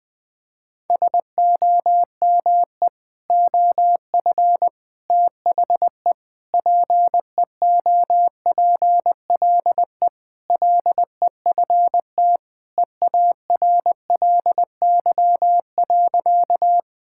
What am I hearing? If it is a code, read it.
Morse code, heard as a keyed tone, about 20 wpm, 700 Hz. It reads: SOME OF THE PEOPLE LEFT EARLY.
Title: Sentences with 3-Word Phrases - 20wpm